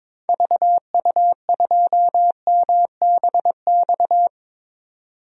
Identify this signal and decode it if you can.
Morse code, heard as a keyed tone, about 21 wpm, 700 Hz. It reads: VU2MBX
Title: DXCC-100 Call Signs - 6 Characters - 22wpm